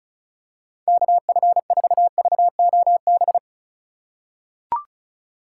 Morse code, heard as a keyed tone, about 34 wpm, 690 Hz. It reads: KF4VOB
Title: USA Call Signs - 4x Character Spacing - 35wpm